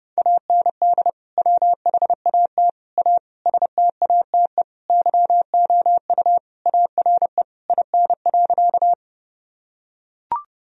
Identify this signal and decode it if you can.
Morse code, heard as a keyed tone, about 30 wpm, 700 Hz. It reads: AND WHAT A STATE YOU ARE IN.
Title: Sentences from Top 1000 Words - Review Part 2 - 30wpm